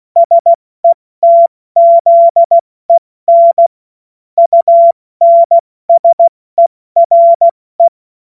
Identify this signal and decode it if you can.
Morse code, heard as a keyed tone, about 16 wpm, spaced as for 14 wpm, 680 Hz. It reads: SETZEN UNSERE
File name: Morsekurs / Witze / Witz 07. Februar 2023 70 F80